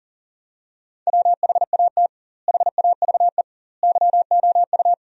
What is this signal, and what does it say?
Morse code, heard as a keyed tone, about 40 wpm, 700 Hz. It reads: WHAT HAVE YOU
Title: 2x Word-Spacing - Sentences from Top 400 Words - 40wpm